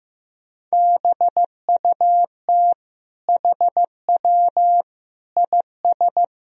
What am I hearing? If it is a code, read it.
Morse code, heard as a keyed tone, about 15 wpm, 700 Hz. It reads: BUT HW IS
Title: Sentences from Top 100 Words and Abbreviations 15wpm